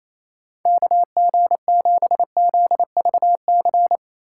Morse code, heard as a keyed tone, about 36 wpm, 700 Hz. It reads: KG7ZVC